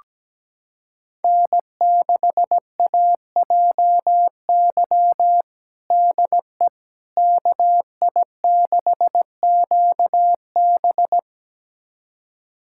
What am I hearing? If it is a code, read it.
Morse code, heard as a keyed tone, about 17 wpm, 700 Hz. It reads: N6AJY DE KI6QB
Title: USA Callsign DE Callsign - 5 Characters - 17wpm